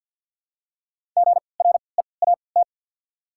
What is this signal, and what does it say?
Morse code, heard as a keyed tone, about 54 wpm, 690 Hz. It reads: G R E A T
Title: 3x Character Spacing - Top 200 Words - 50wpm